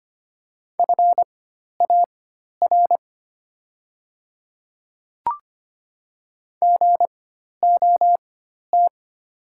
Morse code, heard as a keyed tone, about 25 wpm, spaced as for 12 wpm, 690 Hz. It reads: FAR GOT
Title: 4x Character Spacing - Top 300 Words - 25wpm